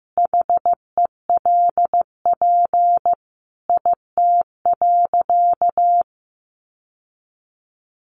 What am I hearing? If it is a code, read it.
Morse code, heard as a keyed tone, about 15 wpm, 700 Hz. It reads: HELP IT.
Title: Sentences from Top 1100 Words 15wpm